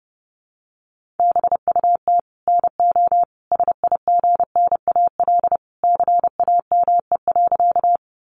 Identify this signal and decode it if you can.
Morse code, heard as a keyed tone, about 30 wpm, 700 Hz. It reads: BUT NO SIGNAL CAME.
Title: Sentences from Top 1400 Words 30wpm